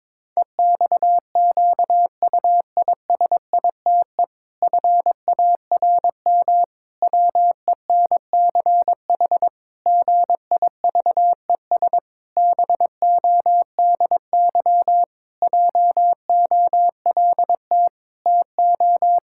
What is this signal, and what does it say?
Morse code, heard as a keyed tone, about 22 wpm, 700 Hz. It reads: EXQUISITE FARM WENCH GIVES BODY JOLT TO
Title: Pangram Challenge v2 - 22wpm